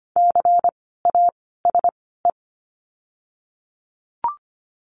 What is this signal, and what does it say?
Morse code, heard as a keyed tone, about 25 wpm, spaced as for 16 wpm, 700 Hz. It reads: CASE
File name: course / 25:10wpm / 094 Lesson 16f - Letter c - Words 25wpm